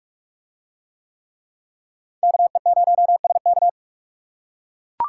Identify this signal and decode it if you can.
Morse code, heard as a keyed tone, about 43 wpm, 700 Hz. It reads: KE0SK